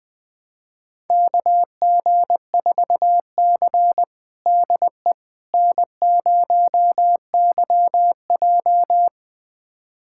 Morse code, heard as a keyed tone, about 20 wpm, 700 Hz. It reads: KG4C DE N0YJ
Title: USA Callsign DE Callsign - 4 Characters - 20wpm